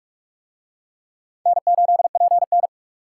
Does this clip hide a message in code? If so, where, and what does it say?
Yes: Morse code from 1.5 to 2.7 s, heard as a keyed tone, about 44 wpm, 700 Hz. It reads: N8PN